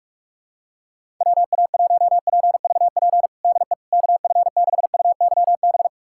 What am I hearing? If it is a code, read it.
Morse code, heard as a keyed tone, about 46 wpm, 700 Hz. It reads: WA1PVP DE KF6UYB
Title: USA Callsign DE Callsign - 6 Characters - 45wpm